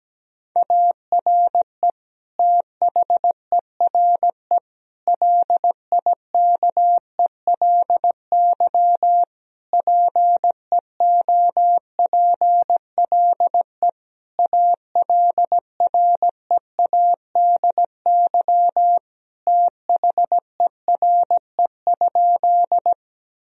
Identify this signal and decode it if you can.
Morse code, heard as a keyed tone, about 17 wpm, 700 Hz. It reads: ARE THERE LIKELY PEOPLE ALREADY THERE?
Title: Sentences from Top 1000 Words - Review Part 4 - 17wpm